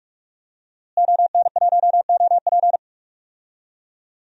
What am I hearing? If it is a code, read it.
Morse code, heard as a keyed tone, about 46 wpm, 700 Hz. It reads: ON1OP
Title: DXCC-100 Call Signs - 5 Characters - Repeated 2x - 45wpm